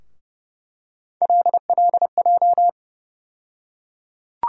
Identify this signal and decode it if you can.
Morse code, heard as a keyed tone, about 30 wpm, 700 Hz. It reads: LLJ
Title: USA Call Sign Postfixes - 3 Letters 30wpm